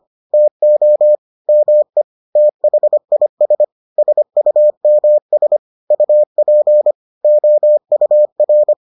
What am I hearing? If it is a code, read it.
Morse code, heard as a keyed tone, about 25 wpm, 600 Hz. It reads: TO ME THIS SUMS UP OUR